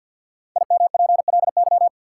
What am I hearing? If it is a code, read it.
Morse code, heard as a keyed tone, about 51 wpm, 700 Hz. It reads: IMPLY
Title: Five Letter Words - 50wpm